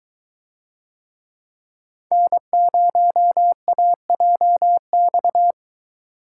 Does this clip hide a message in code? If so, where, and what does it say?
Yes: Morse code from 2.1 to 5.5 s, heard as a keyed tone, about 22 wpm, 700 Hz. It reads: N0AJX